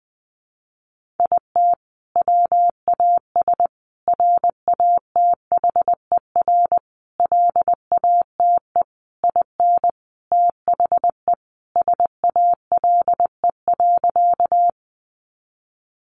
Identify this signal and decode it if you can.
Morse code, heard as a keyed tone, about 19 wpm, 700 Hz. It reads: IT WAS RATHER LATE IN THE SALE.
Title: Sentences from Top 1400 Words 20wpm